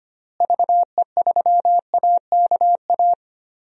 Morse code, heard as a keyed tone, about 25 wpm, 700 Hz. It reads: VE3AKA
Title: DXCC-100 Call Signs - 6 Characters - Repeated 3x - 25wpm